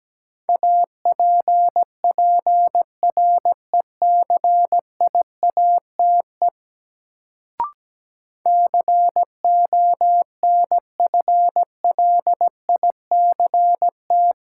Morse code, heard as a keyed tone, about 17 wpm, 700 Hz. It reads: APPRECIATE CONFLICT
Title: Sentences from Top 1300 Words - New Word Drill Alt - 17wpm